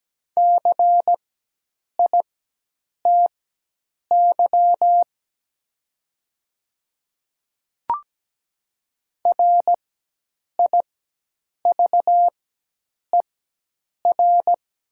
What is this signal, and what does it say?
Morse code, heard as a keyed tone, about 17 wpm, spaced as for 8 wpm, 700 Hz. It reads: CITY RIVER
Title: 4x Character Spacing - Top 500 Words - 17wpm